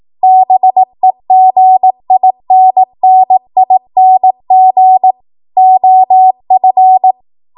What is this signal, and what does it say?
Morse code, heard as a keyed tone, about 18 wpm, 750 Hz. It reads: BEGINNING OF
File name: Morse / 18WPM